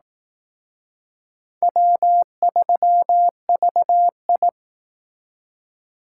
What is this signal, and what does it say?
Morse code, heard as a keyed tone, about 19 wpm, 700 Hz. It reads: W3VI